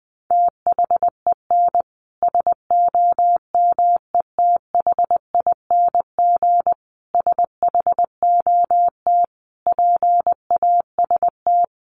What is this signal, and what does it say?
Morse code, heard as a keyed tone, about 20 wpm, 700 Hz. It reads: THEN SOMETHING SHOT PAST
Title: Sentences from Top 1100 Words 20wpm